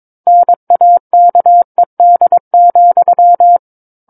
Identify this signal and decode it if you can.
Morse code, heard as a keyed tone, about 22 wpm, 700 Hz. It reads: NAKED,
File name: Book One / 13-Chapter 13 - 22 WPM